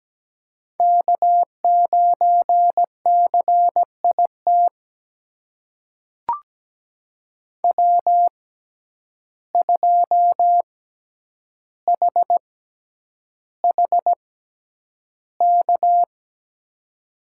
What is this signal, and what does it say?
Morse code, heard as a keyed tone, about 17 wpm, spaced as for 9 wpm, 700 Hz. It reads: K9CIT W2HHK
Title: USA Call Signs - 6x Character Spacing - 17wpm